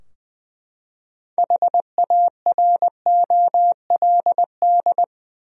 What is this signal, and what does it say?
Morse code, heard as a keyed tone, about 20 wpm, 700 Hz. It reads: HAROLD
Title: USA Common Names - 20wpm